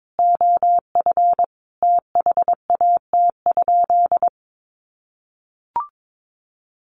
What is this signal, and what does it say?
Morse code, heard as a keyed tone, about 22 wpm, 700 Hz. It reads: OF THAT?
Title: Sentences from Top 1200 Words 22wpm